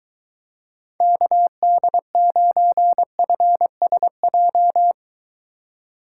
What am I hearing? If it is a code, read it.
Morse code, heard as a keyed tone, about 22 wpm, 700 Hz. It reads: KD9FSJ